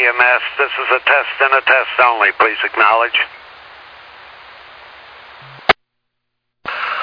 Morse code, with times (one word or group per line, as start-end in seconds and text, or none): none